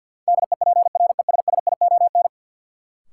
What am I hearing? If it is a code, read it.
Morse code, heard as a keyed tone, about 50 wpm, 700 Hz. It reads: DEPRESSION
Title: Sentences from Top 1300 Words - New Word Drill - 50wpm